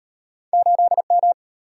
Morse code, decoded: 8M

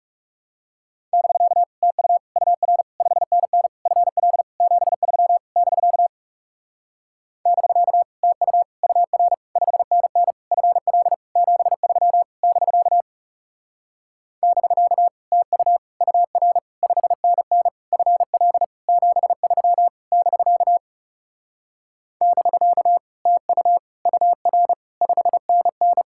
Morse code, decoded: <BK> TU UR 5NN FL 73 <BK> <BK> TU UR 5NN FL 73 <BK> <BK> TU UR 5NN FL 73 <BK> <BK> TU UR 5NN

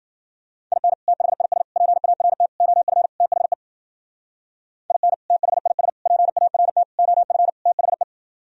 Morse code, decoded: IN THIS PART OF THE IN THIS PART OF THE

